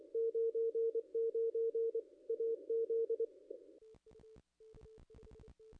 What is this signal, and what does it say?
Morse code, heard as a keyed tone, about 24 wpm, 460 Hz. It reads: 99 AZ
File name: contests / ss-cw-k5tr-op-n5rz -0500